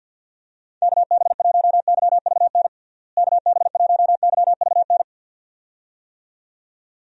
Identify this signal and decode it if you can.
Morse code, heard as a keyed tone, about 49 wpm, 700 Hz. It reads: KB1YVN KB1YVN
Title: USA Call Signs - 6 Characters - Repeated 2x - 50wpm